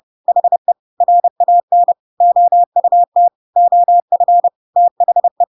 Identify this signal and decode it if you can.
Morse code, heard as a keyed tone, about 30 wpm, 700 Hz. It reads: HE RAN OUT OF THE
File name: Pinocchio / Pinocchio ch6 30 30 ik1hge